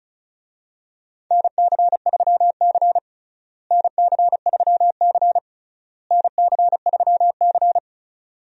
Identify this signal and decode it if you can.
Morse code, heard as a keyed tone, about 36 wpm, 700 Hz. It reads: NC3C NC3C NC3C
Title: USA Call Signs - 4 Characters - Repeated 3x - 35wpm